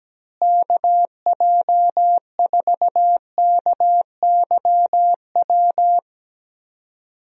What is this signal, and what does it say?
Morse code, heard as a keyed tone, about 17 wpm, 700 Hz. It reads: KJ4KYW
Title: USA Call Signs - 6 Characters - Repeated 3x - 17wpm